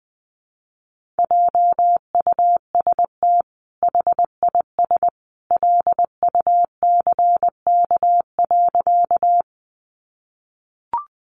Morse code, heard as a keyed tone, about 20 wpm, 700 Hz. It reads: JUST HIS LUCK.
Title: Sentences from Top 1500 Words 20wpm